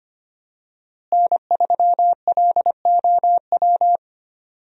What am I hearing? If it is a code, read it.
Morse code, heard as a keyed tone, about 25 wpm, 700 Hz. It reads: N3LOW
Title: USA Call Signs - 5 Characters - Repeated 3x - 25wpm